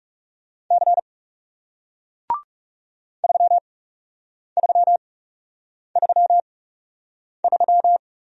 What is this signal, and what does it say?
Morse code, heard as a keyed tone, about 40 wpm, spaced as for 8 wpm, 700 Hz. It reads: C 3333